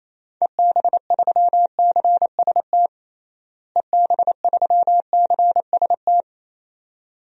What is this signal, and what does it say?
Morse code, heard as a keyed tone, about 28 wpm, 700 Hz. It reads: EB3CST EB3CST
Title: DXCC-100 Call Signs - 6 Characters - Repeated 2x - 28wpm